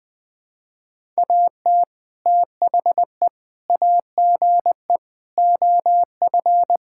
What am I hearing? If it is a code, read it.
Morse code, heard as a keyed tone, about 21 wpm, 700 Hz. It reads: AT THE AGE OF
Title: Sentences with 4-Word Phrases - 20wpm